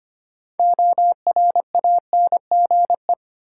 Morse code, heard as a keyed tone, about 25 wpm, 700 Hz. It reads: ORANGE
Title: Six Letter Words - 25wpm